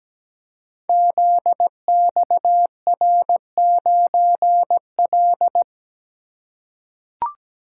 Morse code, heard as a keyed tone, about 17 wpm, 700 Hz. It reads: ZXR9L
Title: Code Groups - 5 Letter:Numbers - 17wpm